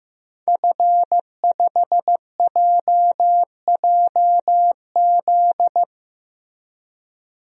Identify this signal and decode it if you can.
Morse code, heard as a keyed tone, about 16 wpm, 700 Hz. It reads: F5JJZ
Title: DXCC-100 Call Signs - 5 Characters - Repeated 3x - 15wpm